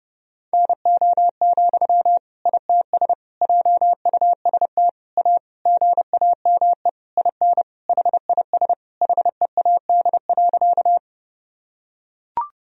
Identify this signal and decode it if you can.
Morse code, heard as a keyed tone, about 30 wpm, 700 Hz. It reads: NO, ITS JUST A GAME IN HIS HEAD.
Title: Sentences from Top 1000 Words - Review Part 3 - 30wpm